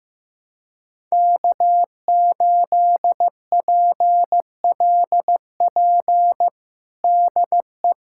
Morse code, heard as a keyed tone, about 16 wpm, 700 Hz. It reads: K8PLP DE
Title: USA Callsign DE Callsign - 5 Characters - 15wpm